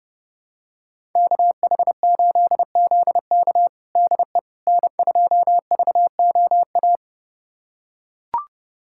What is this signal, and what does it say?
Morse code, heard as a keyed tone, about 30 wpm, 700 Hz. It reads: KH8ZK DE N2VOA
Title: USA Callsign DE Callsign - 5 Characters - 30wpm